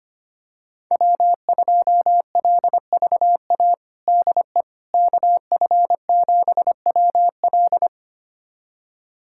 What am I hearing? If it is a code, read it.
Morse code, heard as a keyed tone, about 25 wpm, 700 Hz. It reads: W2LVA DE KF7WL
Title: USA Callsign DE Callsign - 5 Characters - 25wpm